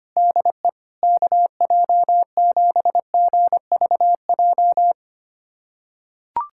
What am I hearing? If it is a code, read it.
Morse code, heard as a keyed tone, about 25 wpm, 700 Hz. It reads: DE KJ7GVJ
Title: USA Callsign DE Callsign 25wpm